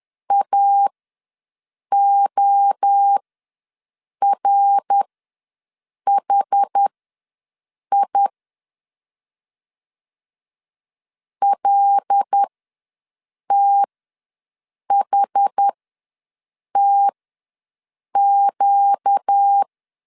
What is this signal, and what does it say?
Morse code, heard as a keyed tone, about 10 wpm, spaced as for 6 wpm, 800 Hz. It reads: AORHI LTHTQ
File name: LEZIONI / 5 LBVC / 5 LBV ws 10wpm 3645